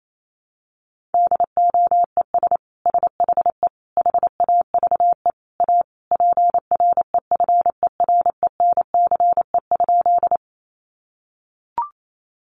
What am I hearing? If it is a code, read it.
Morse code, heard as a keyed tone, about 28 wpm, 700 Hz. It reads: DOES SHE HAVE A PREFERENCE?
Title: Sentences from Top 1500 Words 28wpm